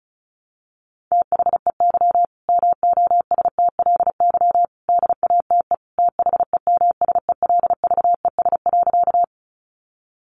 Morse code, heard as a keyed tone, about 34 wpm, 700 Hz. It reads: THEY MOSTLY DATE THEMSELVES.
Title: Sentences from Top 1300 Words 35wpm